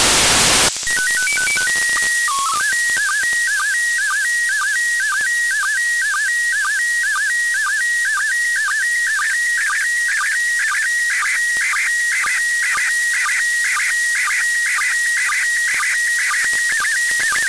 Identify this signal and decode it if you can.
SSTV (slow-scan television), PD 120.
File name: ISS-SSTV-audio z 04-12-14 12-11-2024